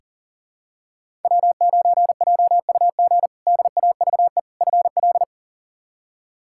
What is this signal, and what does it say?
Morse code, heard as a keyed tone, about 40 wpm, 700 Hz. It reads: W9JUG DAVE FL
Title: K1USN SST QSO Practice - 40wpm